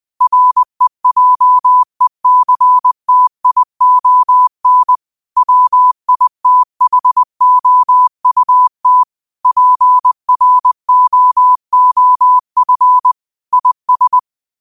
Morse code, heard as a keyed tone, about 20 wpm, 1000 Hz. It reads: REJECTION WITHOUT PROOF IS